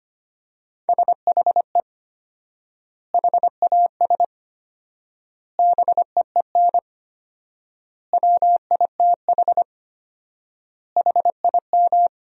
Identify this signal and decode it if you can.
Morse code, heard as a keyed tone, about 25 wpm, 700 Hz. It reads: SHE HAS BEEN WITH HIM
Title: 4x Word-Spacing - Sentences from Top 200 Words - 25wpm